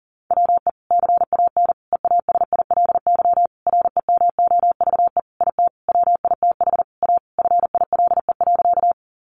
Morse code, heard as a keyed tone, about 40 wpm, 700 Hz. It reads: WE CAN EASILY REMOVE IT WITH A FILE.
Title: Sentences from Top 1200 Words 40wpm